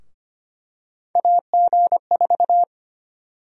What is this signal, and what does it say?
Morse code, heard as a keyed tone, about 25 wpm, 700 Hz. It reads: AG4
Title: USA Call Sign Prefixes - Repeated 2x - 25wpm